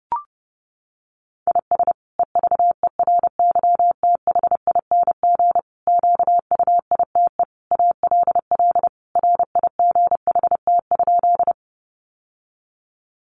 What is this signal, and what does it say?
Morse code, heard as a keyed tone, about 30 wpm, 700 Hz. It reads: IS EVERYTHING QUITE ALL RIGHT?
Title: Sentences from Top 700 Words 30wpm